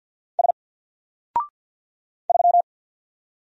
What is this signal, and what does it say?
Morse code, heard as a keyed tone, about 48 wpm, 700 Hz. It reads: S 3
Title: Single Letter-Number Alt 50wpm